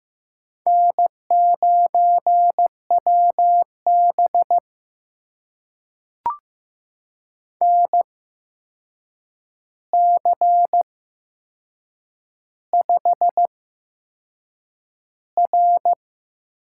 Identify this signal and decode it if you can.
Morse code, heard as a keyed tone, about 15 wpm, spaced as for 6 wpm, 700 Hz. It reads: N9WB NC5R
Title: USA Call Signs - 4 Characters - 8x Character Spacing - 15wpm